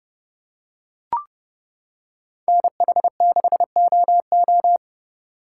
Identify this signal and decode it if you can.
Morse code, heard as a keyed tone, about 29 wpm, 700 Hz. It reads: NH6OO